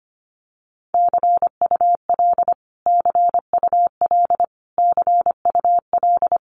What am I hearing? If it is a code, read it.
Morse code, heard as a keyed tone, about 25 wpm, 700 Hz. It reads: CUL CUL CUL
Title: Abbreviations Repeated 3x - 25wpm